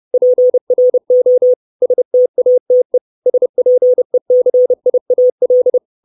Morse code, heard as a keyed tone, about 30 wpm, 500 Hz. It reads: PRO STATE SPECIAL
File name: 632 Words 30WPM